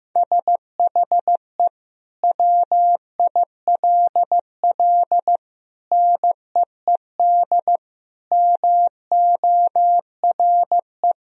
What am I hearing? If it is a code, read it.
Morse code, heard as a keyed tone, about 15 wpm, 700 Hz. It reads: SHE WILL NEED MORE
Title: Sentences from Top 1000 Words - Review Part 4 - 15wpm